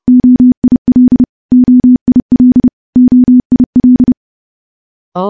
Morse code, heard as a keyed tone, about 30 wpm, 250 Hz. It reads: OIL OIL OIL
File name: morse / morse-training-30WPM